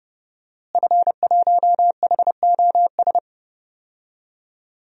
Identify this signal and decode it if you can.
Morse code, heard as a keyed tone, about 30 wpm, 700 Hz. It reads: F1HOS
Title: DXCC-100 Call Signs - 5 Characters - Repeated 2x - 30wpm